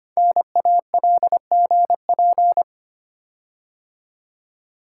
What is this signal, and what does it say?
Morse code, heard as a keyed tone, about 25 wpm, 700 Hz. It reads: NALGP
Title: Code Groups - 5 Letters - 25wpm